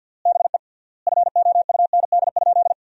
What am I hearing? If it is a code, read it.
Morse code, heard as a keyed tone, about 50 wpm, 700 Hz. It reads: BE FOUND?
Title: 3x Word-Spacing - Sentences from Top 300 Words - 50wpm